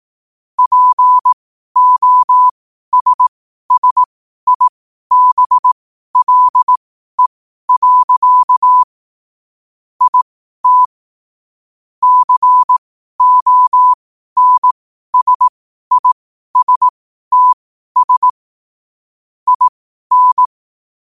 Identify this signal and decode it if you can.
Morse code, heard as a keyed tone, about 18 wpm, spaced as for 12 wpm, 1000 Hz. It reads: POSSIBLE. IT CONSISTS IN
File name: qotdfiles / 12wpm